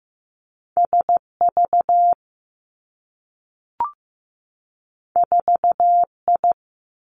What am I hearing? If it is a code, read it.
Morse code, heard as a keyed tone, about 15 wpm, 700 Hz. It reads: S V 4 I
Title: Two Challenging Letter-Number Combinations 15wpm